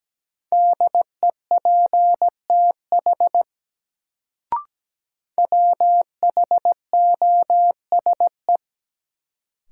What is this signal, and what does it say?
Morse code, heard as a keyed tone, about 17 wpm, 700 Hz. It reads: DEPTH WHOSE
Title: Sentences from Top 1200 Words - New Word Drill - 17wpm